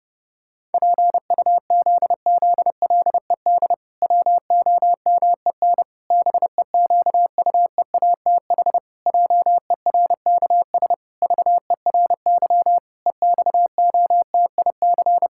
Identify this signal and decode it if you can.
Morse code, heard as a keyed tone, about 30 wpm, 700 Hz. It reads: PUZZLED WOMEN BEQUEATH JERKS VERY EXOTIC
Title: Pangram Challenge v2 - 30wpm